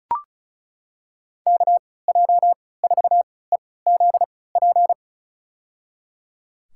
Morse code, heard as a keyed tone, about 34 wpm, spaced as for 20 wpm, 700 Hz. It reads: KJ4EZP
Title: USA Call Signs - 6 Characters - 3x Character Spacing - 35wpm